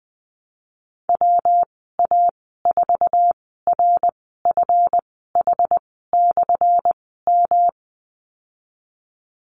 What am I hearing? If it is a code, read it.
Morse code, heard as a keyed tone, about 20 wpm, spaced as for 15 wpm, 700 Hz. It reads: WA4RFH/M
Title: 179 Lesson 29g - Slash - Call Signs 20wpm